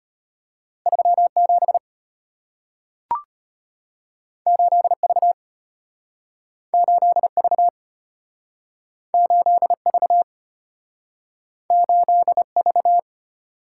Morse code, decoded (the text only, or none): 37 8V 8V 8V 8V